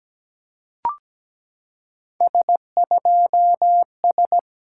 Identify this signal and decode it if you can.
Morse code, heard as a keyed tone, about 16 wpm, 700 Hz. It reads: S2S
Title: SOTA QSO Practice 17wpm